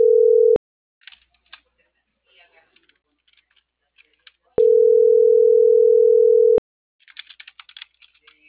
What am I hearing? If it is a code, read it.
Ringback tone (North America), 440 + 480 Hz.
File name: CENTRAL MAERSK / out yunuel.morales